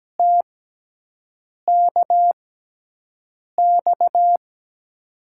Morse code, heard as a keyed tone, about 17 wpm, spaced as for 6 wpm, 700 Hz. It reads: TKX
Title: USA Call Signs - 5 Characters - 6x Character Spacing - 17wpm